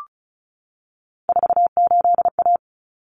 Morse code, heard as a keyed tone, about 34 wpm, 700 Hz. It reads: 48A
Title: Common Cut Numbers - 3-Digit 35wpm